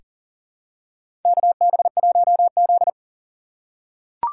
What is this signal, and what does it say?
Morse code, heard as a keyed tone, about 40 wpm, 700 Hz. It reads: KB1Z
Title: USA Call Signs - 3x Character Spacing - 40wpm